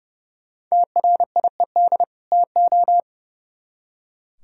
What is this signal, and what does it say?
Morse code, heard as a keyed tone, about 37 wpm, 700 Hz. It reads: TRIED TO
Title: Speed-Racing - Sets of 2 Words 30wpm